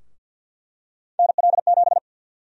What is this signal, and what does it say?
Morse code, heard as a keyed tone, about 51 wpm, 700 Hz. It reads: NL7
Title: USA Call Sign Prefixes - 50wpm